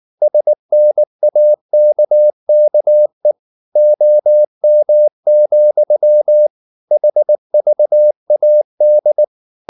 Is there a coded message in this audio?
Yes: Morse code from 0.2 to 9.2 s, heard as a keyed tone, about 19 wpm, 600 Hz. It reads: SNAKKE OM, HVAD